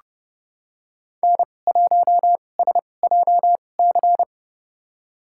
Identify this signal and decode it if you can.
Morse code, heard as a keyed tone, about 30 wpm, spaced as for 22 wpm, 700 Hz. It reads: N1SJC
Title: USA Call Signs - 2x Character Spacing - 30wpm